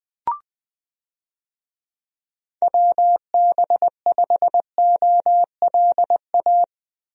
Morse code, heard as a keyed tone, about 21 wpm, 700 Hz. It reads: WB5OLA
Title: USA Call Signs - 6 Characters - Repeated 2x - 20wpm